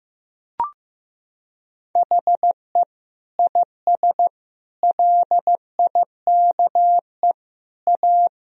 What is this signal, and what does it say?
Morse code, heard as a keyed tone, about 16 wpm, 700 Hz. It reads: HE IS LIKE A
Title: Sentences from Top 1000 Words - Review Part 1 - 15wpm